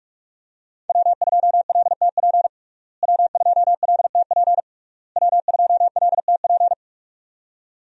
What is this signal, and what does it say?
Morse code, heard as a keyed tone, about 46 wpm, 700 Hz. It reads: W2LTP W2LTP W2LTP
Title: USA Call Signs - 5 Characters - Repeated 3x - 45wpm